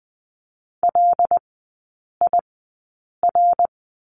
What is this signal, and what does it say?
Morse code, heard as a keyed tone, about 20 wpm, spaced as for 8 wpm, 700 Hz. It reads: LIR
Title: 152 Lesson 25e - Letter p - Three Characters 20wpm